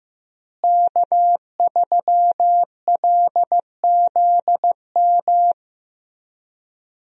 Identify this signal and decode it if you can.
Morse code, heard as a keyed tone, about 15 wpm, 700 Hz. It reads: K3LZM
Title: USA Call Signs - 5 Characters - Repeated 3x - 15wpm